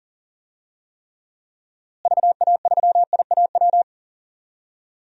Morse code, heard as a keyed tone, about 40 wpm, 700 Hz. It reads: VA3IAW
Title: DXCC-100 Call Signs - 6 Characters - Repeated 3x - 40wpm